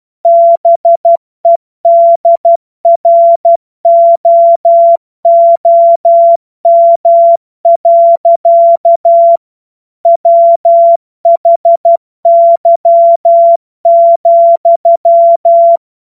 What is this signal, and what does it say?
Morse code, heard as a keyed tone, about 12 wpm, 680 Hz. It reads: BEDROOM. WHY,
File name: Morsekurs / Witze